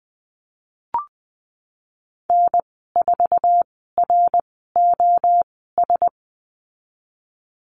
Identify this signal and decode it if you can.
Morse code, heard as a keyed tone, about 19 wpm, spaced as for 14 wpm, 700 Hz. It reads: N4ROS
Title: 082 Lesson 14g - Number 2 - Call Signs 20wpm